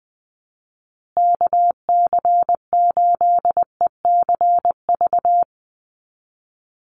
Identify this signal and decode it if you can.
Morse code, heard as a keyed tone, about 19 wpm, 700 Hz. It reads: KC8ECV
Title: USA Call Signs Repeated 2x 20wpm